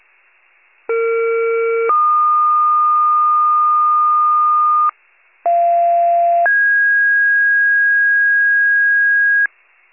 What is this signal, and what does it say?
Two-tone page, heard as a one-second tone, then a three-second tone, about 460 then 1220 Hz, about 690 then 1650 Hz, with noise. It is heard from 0.9 to 9.5 s.
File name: pager / 41 RESCUE DUTY / wcalert